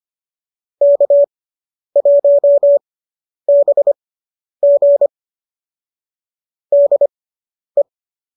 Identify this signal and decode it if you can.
Morse code, heard as a keyed tone, about 25 wpm, spaced as for 10 wpm, 590 Hz. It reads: K1BG DE